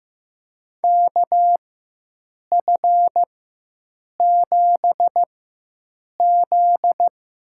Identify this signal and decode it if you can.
Morse code, heard as a keyed tone, about 15 wpm, spaced as for 7 wpm, 700 Hz. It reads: KF7Z
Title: USA Call Signs - 4x Character Spacing - 15wpm